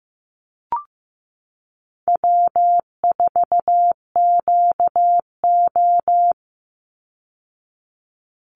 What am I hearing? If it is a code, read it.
Morse code, heard as a keyed tone, about 15 wpm, 700 Hz. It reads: W4QO
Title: Easy Call Signs Repeated 2x - 15wpm